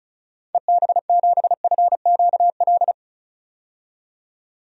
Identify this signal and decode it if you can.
Morse code, heard as a keyed tone, about 36 wpm, 700 Hz. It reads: EB7FQL
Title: DXCC-100 Call Signs - 6 Characters - 35wpm